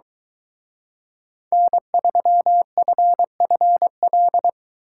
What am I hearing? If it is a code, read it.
Morse code, heard as a keyed tone, about 23 wpm, 700 Hz. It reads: N3FFL